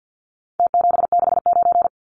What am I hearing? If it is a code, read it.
Morse code, heard as a keyed tone, about 50 wpm, 700 Hz. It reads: T769
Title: Numbers - Contest Style - 4-Digit 50wpm